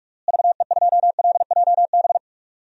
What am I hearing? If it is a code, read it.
Morse code, heard as a keyed tone, about 46 wpm, 700 Hz. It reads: VE2LJB
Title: DXCC-100 Call Signs - 6 Characters - Repeated 2x - 45wpm